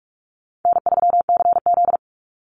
Morse code, heard as a keyed tone, about 44 wpm, 700 Hz. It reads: N3CZ
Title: Easy Call Signs - 45wpm